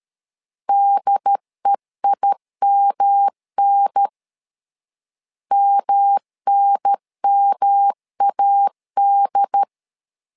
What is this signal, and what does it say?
Morse code, heard as a keyed tone, about 13 wpm, 800 Hz. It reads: DEIMN MNMAD
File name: LEZIONI / 3 MWRO / 3 M 12wpm 1541